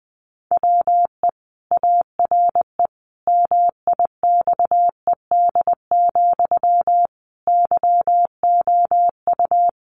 Morse code, heard as a keyed tone, about 20 wpm, 700 Hz. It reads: WE ARE MIXED, YOU
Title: Sentences from Top 1500 Words 20wpm